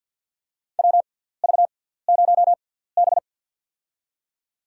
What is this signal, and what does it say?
Morse code, heard as a keyed tone, about 50 wpm, spaced as for 17 wpm, 690 Hz. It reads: WV0B
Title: USA Call Signs - 4 Characters - 6x Character Spacing - 50wpm